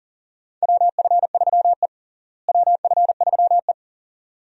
Morse code, heard as a keyed tone, about 37 wpm, 700 Hz. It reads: WF3E WF3E
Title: USA Call Signs - 4 Characters - Repeated 2x - 40wpm